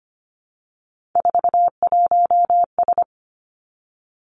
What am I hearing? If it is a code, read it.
Morse code, heard as a keyed tone, about 25 wpm, 700 Hz. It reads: 41S
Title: Three Challenging Letter-Number Combinations 25wpm